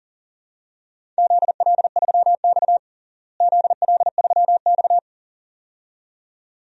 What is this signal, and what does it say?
Morse code, heard as a keyed tone, about 37 wpm, 700 Hz. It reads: ZL3X ZL3X
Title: DXCC-100 Call Signs - 4 Characters - Repeated 2x - 40wpm